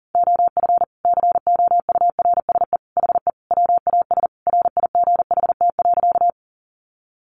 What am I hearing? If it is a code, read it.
Morse code, heard as a keyed tone, about 40 wpm, 700 Hz. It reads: OF COURSE HE WAS RIGHT.
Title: Sentences from Top 500 Words 40wpm